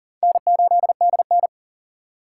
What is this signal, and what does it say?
Morse code, heard as a keyed tone, about 49 wpm, 700 Hz. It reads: N8DN
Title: Speed Racing - USA 1x2 Call Signs 40wpm